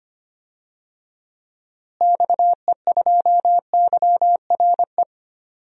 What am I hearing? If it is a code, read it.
Morse code, heard as a keyed tone, about 25 wpm, 700 Hz. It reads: XE2YRE
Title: DXCC-100 Call Signs - 6 Characters - Repeated 2x - 25wpm